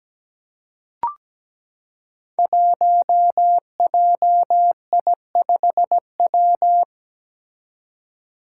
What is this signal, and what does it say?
Morse code, heard as a keyed tone, about 17 wpm, 700 Hz. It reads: 1JI5W
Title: Five Challenging Letter-Number Combinations 17wpm